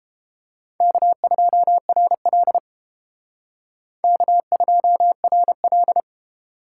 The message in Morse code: K2RL K2RL

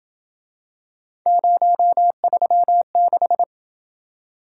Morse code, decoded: 036

